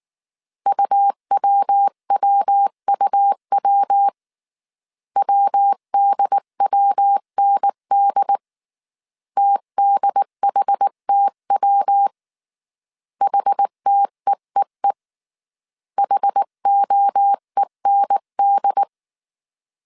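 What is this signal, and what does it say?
Morse code, heard as a keyed tone, about 19 wpm, 800 Hz. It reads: UWWUW WDWND TDHTW HTEEE HOEND
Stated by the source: Morse code dictation at 18 wpm from the IW3SGT QRS Morse code training course, five-character groups of 300 characters.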